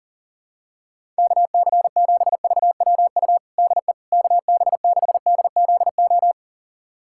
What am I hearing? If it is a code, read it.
Morse code, heard as a keyed tone, about 40 wpm, 700 Hz. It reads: KC7VWU DE KB6DZO
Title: USA Callsign DE Callsign 40wpm